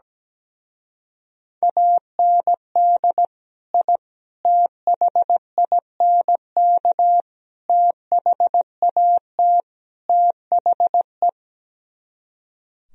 Morse code, heard as a keyed tone, about 17 wpm, 700 Hz. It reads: AND I THINK THAT THE
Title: Speed-Racing - Sets of 5 Words 17wpm